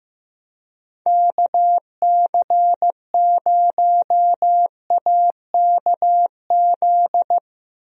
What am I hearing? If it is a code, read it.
Morse code, heard as a keyed tone, about 15 wpm, 700 Hz. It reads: KC0AKZ
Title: USA Call Signs - 6 Characters - Repeated 2x - 15wpm